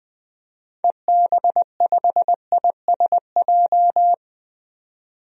Morse code, heard as a keyed tone, about 20 wpm, 700 Hz. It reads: EB5ISJ
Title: DXCC-100 Call Signs - 6 Characters - 20wpm